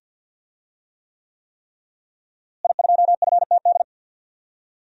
Morse code, heard as a keyed tone, about 48 wpm, 700 Hz. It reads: I2FTD
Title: DXCC-100 Call Signs - 5 Characters - Repeated 3x - 50wpm